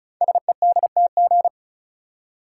Morse code, heard as a keyed tone, about 36 wpm, 700 Hz. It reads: SEDTG
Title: Code Groups - 5 Letter:Numbers - 35wpm